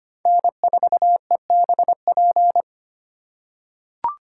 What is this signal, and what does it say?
Morse code, heard as a keyed tone, about 25 wpm, 700 Hz. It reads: N4EBP
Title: USA Call Signs - 5 Characters - 4x Character Spacing - 25wpm